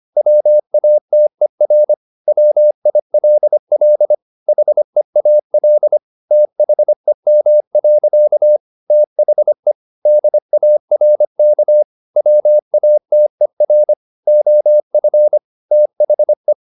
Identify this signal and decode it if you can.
Morse code, heard as a keyed tone, about 25 wpm, 600 Hz. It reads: WATER WILL HEAL THEM. THE DARK WATER OF THE